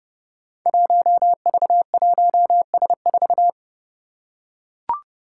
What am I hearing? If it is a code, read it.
Morse code, heard as a keyed tone, about 32 wpm, 700 Hz. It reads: 1V1S4
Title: Five Challenging Letter-Number Combinations 30wpm